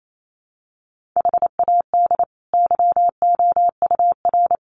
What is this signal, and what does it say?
Morse code, heard as a keyed tone, about 28 wpm, 700 Hz. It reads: HAD YOUR
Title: Sentences from Top 1500 Words 28wpm